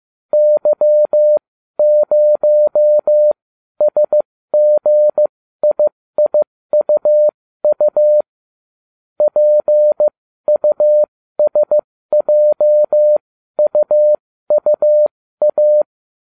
Morse code, heard as a keyed tone, about 15 wpm, spaced as for 12 wpm, 600 Hz. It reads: Y0SGIIUU PUSJUUA